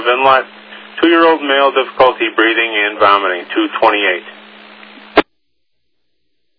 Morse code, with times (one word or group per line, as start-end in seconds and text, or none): none